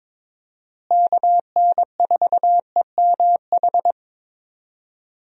Morse code, heard as a keyed tone, about 22 wpm, 700 Hz. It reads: KN4EMH
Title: USA Call Signs - 6 Characters - Repeated 3x - 22wpm